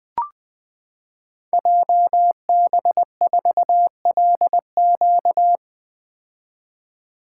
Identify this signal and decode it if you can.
Morse code, heard as a keyed tone, about 21 wpm, 700 Hz. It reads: JB4LQ
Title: Code Groups - 5 Letter:Numbers - 20wpm